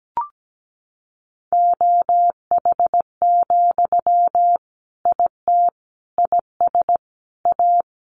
Morse code, heard as a keyed tone, about 17 wpm, 700 Hz. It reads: OH, IT IS A
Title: Sentences from Top 1200 Words 17wpm